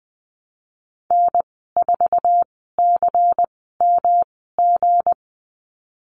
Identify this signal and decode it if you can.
Morse code, heard as a keyed tone, about 19 wpm, spaced as for 14 wpm, 700 Hz. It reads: N4CMG
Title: 179 Lesson 29g - Slash - Call Signs 20wpm